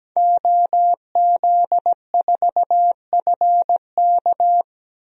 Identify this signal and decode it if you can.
Morse code, heard as a keyed tone, about 17 wpm, 700 Hz. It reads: OZ4FK
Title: DXCC-100 Call Signs - 5 Characters - Repeated 2x - 17wpm